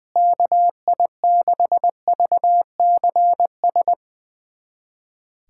Morse code, heard as a keyed tone, about 21 wpm, 700 Hz. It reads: KI6VCS
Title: USA Callsign DE Callsign - 6 Characters - 20wpm